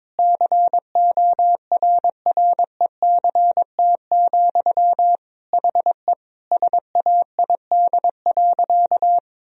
Morse code, heard as a keyed tone, about 21 wpm, 700 Hz. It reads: CORRECT, HE SAID.
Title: Sentences from Top 1000 Words - Review Part 2 - 22wpm